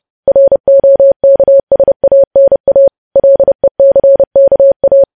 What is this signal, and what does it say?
Morse code, heard as a keyed tone, about 30 wpm, 550 Hz. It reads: ROKSANA LECKA
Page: news headlines in morse code at 30 wpm